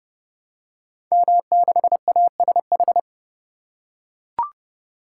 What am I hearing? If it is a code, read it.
Morse code, heard as a keyed tone, about 30 wpm, 700 Hz. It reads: M6ASH
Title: Code Groups - 5 Letter:Numbers - 30wpm